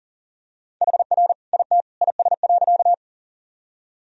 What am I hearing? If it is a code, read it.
Morse code, heard as a keyed tone, about 40 wpm, 700 Hz. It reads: HR IT IS.
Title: Sentences from Top 200 Words and Abbreviations 40wpm